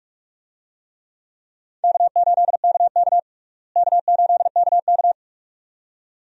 Morse code, heard as a keyed tone, about 46 wpm, 700 Hz. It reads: K8KK K8KK
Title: USA Call Signs - 4 Characters - Repeated 2x - 45wpm